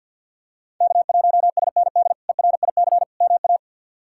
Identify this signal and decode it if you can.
Morse code, heard as a keyed tone, about 53 wpm, 700 Hz. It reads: K1SND ERIC MA
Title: K1USN SST QSO Practice - 50wpm